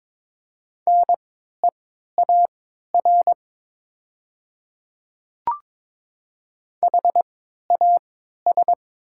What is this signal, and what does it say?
Morse code, heard as a keyed tone, about 22 wpm, spaced as for 12 wpm, 700 Hz. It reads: NEAR HAS
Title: 3x Character Spacing - Top 200 Words - 22wpm